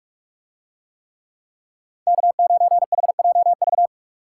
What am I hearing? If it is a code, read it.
Morse code, heard as a keyed tone, about 43 wpm, 700 Hz. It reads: K9HJV